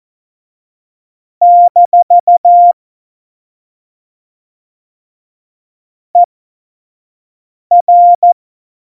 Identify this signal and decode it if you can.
Morse code, heard as a keyed tone, about 14 wpm, spaced as for 5 wpm, 700 Hz. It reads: - ER
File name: Morsekurs / Witze / Witz 03.Juli 2018, 25 F70